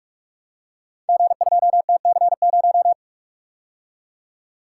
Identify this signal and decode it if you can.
Morse code, heard as a keyed tone, about 44 wpm, 700 Hz. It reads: G2TC0
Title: Code Groups - 5 Letter:Numbers - 45wpm